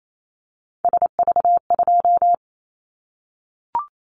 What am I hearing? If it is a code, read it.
Morse code, heard as a keyed tone, about 28 wpm, 700 Hz. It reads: SV2
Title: Callsign Prefixes for Top 50 DXCC - 28wpm